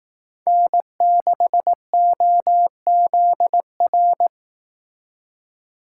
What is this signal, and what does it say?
Morse code, heard as a keyed tone, about 18 wpm, 700 Hz. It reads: N6OZR
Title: Speed Racing - USA 1x3 Call Signs 15wpm